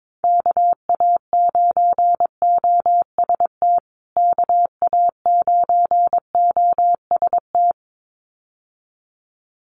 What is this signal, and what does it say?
Morse code, heard as a keyed tone, about 22 wpm, 700 Hz. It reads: KA9OST KA9OST
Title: USA Call Signs Repeated 2x 22wpm